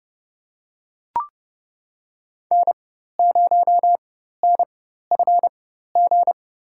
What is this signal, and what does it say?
Morse code, heard as a keyed tone, about 30 wpm, spaced as for 14 wpm, 700 Hz. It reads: N0NFG
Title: USA Call Signs - 4x Character Spacing - 30wpm